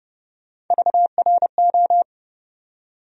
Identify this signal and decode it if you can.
Morse code, heard as a keyed tone, about 30 wpm, 700 Hz. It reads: VRO
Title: USA Call Sign Postfixes - 3 Letters 30wpm